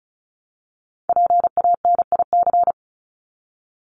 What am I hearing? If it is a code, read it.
Morse code, heard as a keyed tone, about 34 wpm, 700 Hz. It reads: PANIC
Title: Top 1751-2000 Words 35wpm